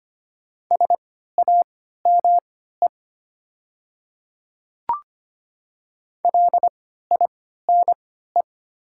Morse code, heard as a keyed tone, about 25 wpm, spaced as for 14 wpm, 690 Hz. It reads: SAME LINE
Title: 3x Character Spacing - Top 500 Words - 25wpm